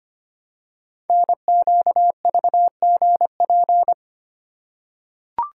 Morse code, heard as a keyed tone, about 25 wpm, 700 Hz. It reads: NQVGP
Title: Code Groups - 5 Letters - 25wpm